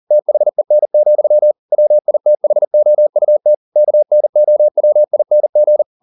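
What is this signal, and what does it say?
Morse code, heard as a keyed tone, about 40 wpm, 600 Hz. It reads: THEN, WITHOUT KNOWING
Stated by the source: Morse code at 40 wpm, practice text, CW med Gnister.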